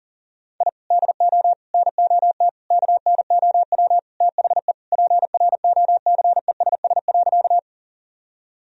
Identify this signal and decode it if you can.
Morse code, heard as a keyed tone, about 40 wpm, 700 Hz. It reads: I DO NOT KNOW THE PROCESS.
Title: Sentences from Top 1000 Words - Review Part 3 - 40wpm